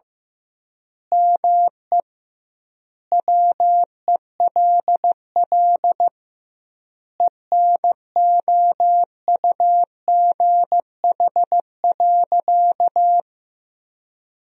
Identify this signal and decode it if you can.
Morse code, heard as a keyed tone, about 15 wpm, 700 Hz. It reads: ME WELL ENOUGH.
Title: 2x Word-Spacing - Sentences from Top 400 Words - 15wpm